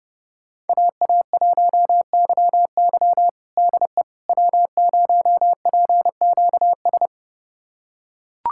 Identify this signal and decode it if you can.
Morse code, heard as a keyed tone, about 30 wpm, 700 Hz. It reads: AA1YY DE W0PQS
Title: USA Callsign DE Callsign - 5 Characters - 30wpm